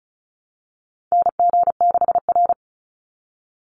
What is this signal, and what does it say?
Morse code, heard as a keyed tone, about 34 wpm, 700 Hz. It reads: NG6R
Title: NA SOTA Chaser Callsigns 35wpm